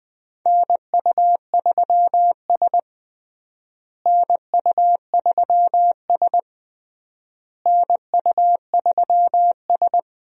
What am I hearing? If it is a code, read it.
Morse code, heard as a keyed tone, about 21 wpm, 700 Hz. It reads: NU3S NU3S NU3S
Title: USA Call Signs - 4 Characters - Repeated 3x - 20wpm